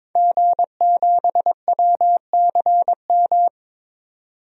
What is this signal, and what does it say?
Morse code, heard as a keyed tone, about 22 wpm, 700 Hz. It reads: G7WCM
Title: DXCC-100 Call Signs - 5 Characters - 22wpm